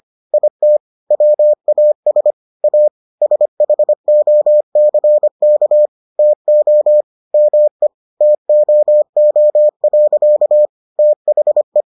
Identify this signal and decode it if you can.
Morse code, heard as a keyed tone, about 25 wpm, 600 Hz. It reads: IT WAS A SHOCK TO ME TOO. THE